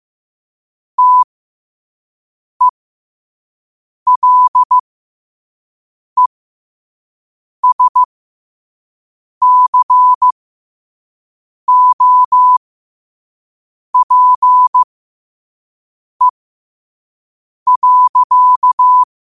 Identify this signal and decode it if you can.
Morse code, heard as a keyed tone, about 15 wpm, spaced as for 5 wpm, 1000 Hz. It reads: TELESCOPE.
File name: 2024-10-07-5wpm